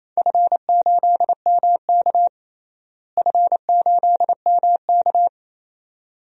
Morse code, heard as a keyed tone, about 27 wpm, 700 Hz. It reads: F8MK F8MK
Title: DXCC-100 Call Signs - 4 Characters - Repeated 2x - 28wpm